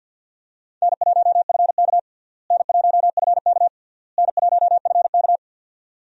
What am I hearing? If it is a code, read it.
Morse code, heard as a keyed tone, about 51 wpm, 700 Hz. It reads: N1FK N1FK N1FK
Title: USA Call Signs - 4 Characters - Repeated 3x - 50wpm